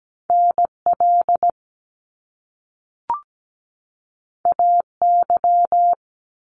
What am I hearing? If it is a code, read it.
Morse code, heard as a keyed tone, about 17 wpm, 700 Hz. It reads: NL AY
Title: Top Two Letter Combinations in Words 17wpm